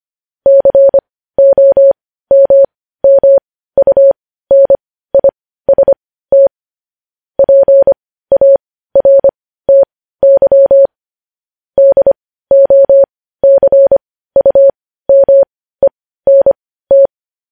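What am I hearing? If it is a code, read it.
Morse code, heard as a keyed tone, about 25 wpm, spaced as for 15 wpm, 550 Hz. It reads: COMMUNIST PARTY DOCUMENT
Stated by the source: Morse code transcription of news headlines.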